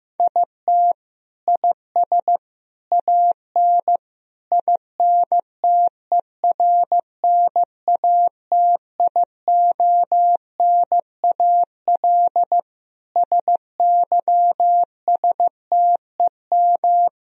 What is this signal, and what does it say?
Morse code, heard as a keyed tone, about 15 wpm, 700 Hz. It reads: IT IS AN INTERNATIONAL SYSTEM
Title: Sentences from Top 1000 Words - Review Part 4 - 15wpm